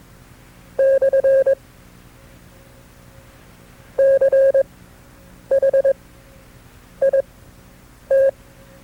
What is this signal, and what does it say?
Morse code, heard as a keyed tone, about 21 wpm, spaced as for 8 wpm, 540 Hz. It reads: / CH IT
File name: FreeFiles / CW-practice / KB6MT-tape2-side2